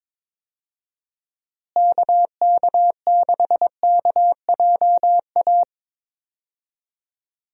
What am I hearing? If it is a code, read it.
Morse code, heard as a keyed tone, about 22 wpm, 700 Hz. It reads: KK6KJA
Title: USA Call Signs - 6 Characters - Repeated 3x - 22wpm